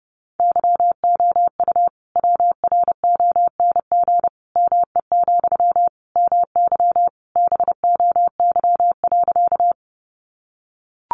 Morse code, heard as a keyed tone, about 30 wpm, 700 Hz. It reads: YOU WRONG ME, MY BOY.
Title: Sentences from Top 900 Words 30wpm